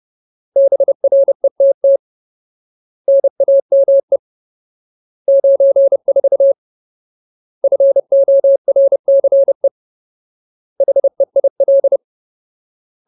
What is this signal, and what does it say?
Morse code, heard as a keyed tone, about 30 wpm, 550 Hz. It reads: BRETT NAME 94 FORCE HEIL
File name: Ultimate Word List / 30WPM Ultimate